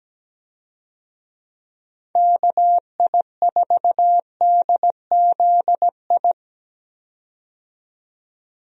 Morse code, heard as a keyed tone, about 17 wpm, 700 Hz. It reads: KI4DZI
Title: USA Call Signs - 6 Characters - 17wpm